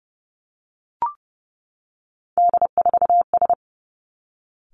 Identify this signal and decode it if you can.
Morse code, heard as a keyed tone, about 30 wpm, 700 Hz. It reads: D4S
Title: Three Challenging Letter-Number Combinations 30wpm